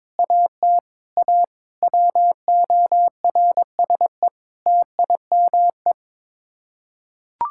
Morse code, heard as a keyed tone, about 21 wpm, 700 Hz. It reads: AT A WORSE TIME
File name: practice / 22wpm / Sets of 8 Words - 22wpm